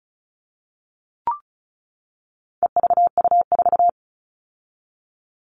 Morse code, heard as a keyed tone, about 34 wpm, 700 Hz. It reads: EVU4
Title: All Cut Numbers - 4-Digit 35wpm